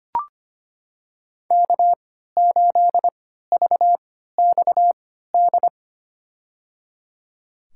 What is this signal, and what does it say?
Morse code, heard as a keyed tone, about 25 wpm, spaced as for 14 wpm, 700 Hz. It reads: K8VXD
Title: USA Call Signs - 5 Characters - 3x Character Spacing - 25wpm